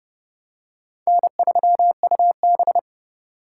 Morse code, heard as a keyed tone, about 30 wpm, 700 Hz. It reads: N3UB
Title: USA Call Signs - 4 Characters - 4x Character Spacing - 30wpm